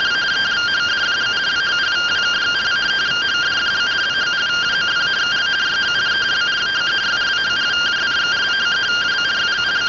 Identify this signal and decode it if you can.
Two-tone teleprinter signal, about 170 Hz shift.